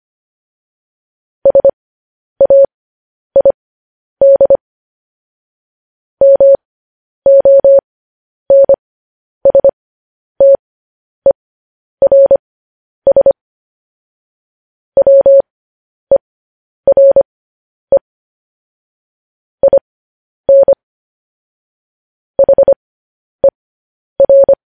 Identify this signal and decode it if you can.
Morse code, heard as a keyed tone, about 25 wpm, spaced as for 10 wpm, 550 Hz. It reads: SAID MONSTERS WERE IN HER
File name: podcast / News headlines-1x-25x10